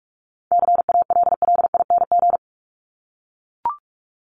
Morse code, heard as a keyed tone, about 44 wpm, 700 Hz. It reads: CALLING
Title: ING Suffix - Top 100 Words with Base Words 45wpm